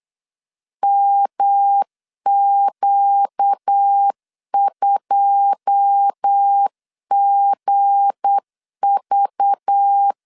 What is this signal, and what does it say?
Morse code, heard as a keyed tone, about 8 wpm, 800 Hz. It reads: MQ2GV
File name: CW / DET / 1 9 1500C S B 8wpm 5624